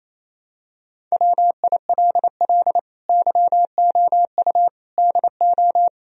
Morse code, heard as a keyed tone, about 28 wpm, 700 Hz. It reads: WILL YOU DO
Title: Sentences from Top 1000 Words - Review Part 3 - 28wpm